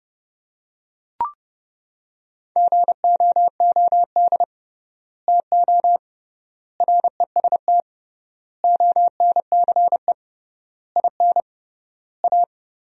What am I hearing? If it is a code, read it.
Morse code, heard as a keyed tone, about 30 wpm, 700 Hz. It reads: GOOD TO REST ONCE IN A
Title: 3x Word-Spacing - Sentences from Top 500 Words - 30wpm